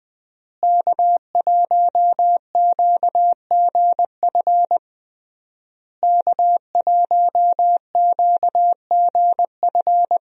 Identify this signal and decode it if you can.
Morse code, heard as a keyed tone, about 20 wpm, 700 Hz. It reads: K1QGF K1QGF
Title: USA Call Signs - 5 Characters - Repeated 3x - 20wpm